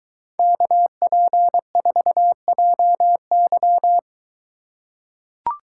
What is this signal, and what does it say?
Morse code, heard as a keyed tone, about 22 wpm, 700 Hz. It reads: KP4JY